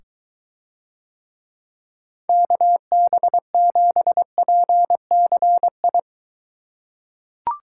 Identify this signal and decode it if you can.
Morse code, heard as a keyed tone, about 23 wpm, 700 Hz. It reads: KB7PCI